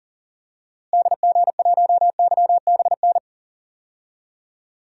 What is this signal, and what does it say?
Morse code, heard as a keyed tone, about 40 wpm, 700 Hz. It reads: DG1YBN
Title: DXCC-100 Call Signs - 6 Characters - Repeated 2x - 40wpm